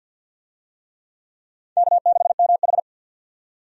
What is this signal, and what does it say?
Morse code, heard as a keyed tone, about 52 wpm, 700 Hz. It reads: K6MH
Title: USA Call Signs - 4 Characters - 50wpm